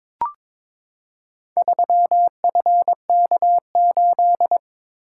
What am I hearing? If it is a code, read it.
Morse code, heard as a keyed tone, about 22 wpm, 700 Hz. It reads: 3FK8
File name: practice / 22wpm / Four Letter-Number Combinations - 22wpm